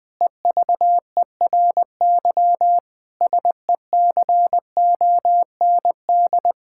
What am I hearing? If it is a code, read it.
Morse code, heard as a keyed tone, about 20 wpm, 700 Hz. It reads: EVERY SECOND
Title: Difficult Sentences with Hints - 20wpm